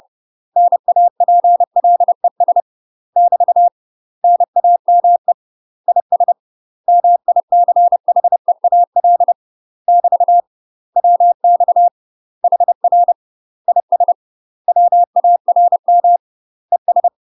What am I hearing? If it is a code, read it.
Morse code, heard as a keyed tone, about 30 wpm, 700 Hz. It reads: NAPLES <BT> NAME IS MICHEAL <BT> WX HR IS WARM ES
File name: UK MASTER QSOs TRAINING / 30WPM / UK MASTER QSOs 0041